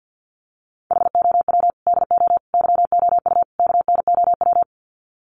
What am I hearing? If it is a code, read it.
Morse code, heard as a keyed tone, about 50 wpm, 700 Hz. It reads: HOW DO YOU KNOW
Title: Sets of 4 Words 50wpm